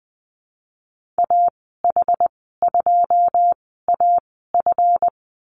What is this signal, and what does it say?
Morse code, heard as a keyed tone, about 20 wpm, spaced as for 15 wpm, 700 Hz. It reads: AH2AF
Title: 186 Lesson 30g - Letter b - Call Signs 20wpm